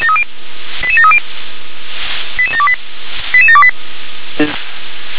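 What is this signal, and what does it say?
Five-tone selective call (ZVEI), heard as quick runs of five tones, tones of about 70 ms. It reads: 88730 88728 88728
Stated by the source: ZVEI five-tone selective call received via SDR.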